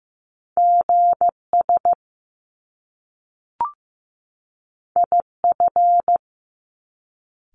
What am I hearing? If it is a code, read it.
Morse code, heard as a keyed tone, about 15 wpm, 700 Hz. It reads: GS IF
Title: Top Two Letter Combinations in Words 15wpm